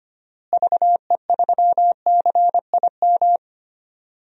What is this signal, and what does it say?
Morse code, heard as a keyed tone, about 25 wpm, 700 Hz. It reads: VE3CIM
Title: DXCC-100 Call Signs - 6 Characters - Repeated 2x - 25wpm